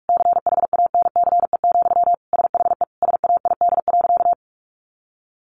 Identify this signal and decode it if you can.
Morse code, heard as a keyed tone, about 44 wpm, 700 Hz. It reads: CHANCE, SHE SAID.
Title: Sentences from Top 800 Words 45wpm